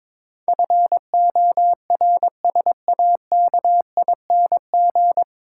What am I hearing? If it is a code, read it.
Morse code, heard as a keyed tone, about 22 wpm, 700 Hz. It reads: FORSAKING
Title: Pangram Challenge v2 - 22wpm